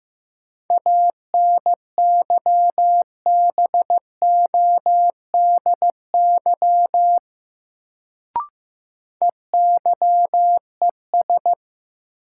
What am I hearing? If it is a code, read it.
Morse code, heard as a keyed tone, about 15 wpm, 700 Hz. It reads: ANYBODY EYES
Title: Sets of 5 Words - New Word Drill - 15wpm